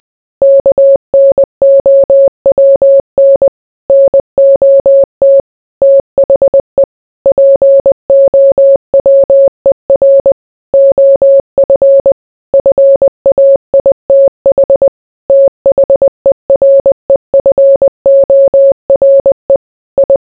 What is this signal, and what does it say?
Morse code, heard as a keyed tone, about 20 wpm, 550 Hz. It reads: KNOWN NOT THE POWER OF FAITH THEREFORE I